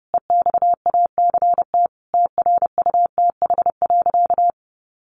Morse code, heard as a keyed tone, about 30 wpm, 700 Hz. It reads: EXACT TRUTH.
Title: Sentences from Top 1300 Words 30wpm